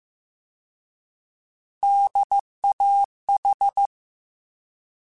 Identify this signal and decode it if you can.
Morse code, heard as a keyed tone, about 15 wpm, 800 Hz. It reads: DAH
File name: Alphabetical / Alphabetical mp3s / Ds @13